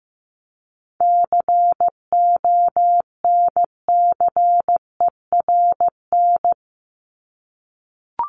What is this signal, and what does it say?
Morse code, heard as a keyed tone, about 15 wpm, 700 Hz. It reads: CONCERN
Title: ED Suffix - Top 100 Words with Base Words 15wpm